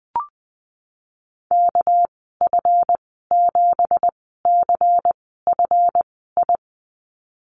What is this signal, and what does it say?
Morse code, heard as a keyed tone, about 20 wpm, spaced as for 14 wpm, 700 Hz. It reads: KF7CFI
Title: 231 Lesson 37g - Letter x - Call Signs 20wpm